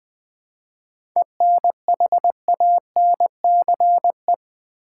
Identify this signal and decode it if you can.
Morse code, heard as a keyed tone, about 20 wpm, 700 Hz. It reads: ENHANCE
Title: Seven Letter Words - 20wpm